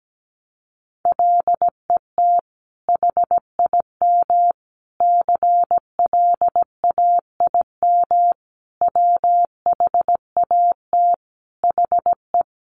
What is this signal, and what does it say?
Morse code, heard as a keyed tone, about 17 wpm, 700 Hz. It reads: LET HIM CLAIM WHAT HE
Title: Sentences from Top 1500 Words 17wpm